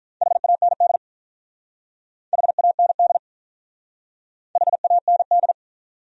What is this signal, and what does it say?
Morse code, heard as a keyed tone, about 46 wpm, 700 Hz. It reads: HAND HAND HAND